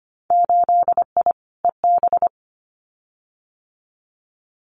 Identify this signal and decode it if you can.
Morse code, heard as a keyed tone, about 25 wpm, 700 Hz. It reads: 8I EB
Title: Winter Field Day Exchanges Encore 25wpm